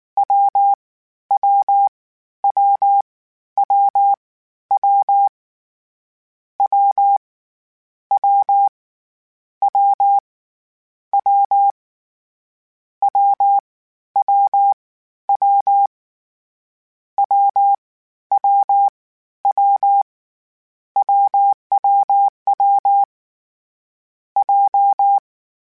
Morse code, heard as a keyed tone, about 19 wpm, 800 Hz. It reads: W W W W W W W W W W W W W W W WWW J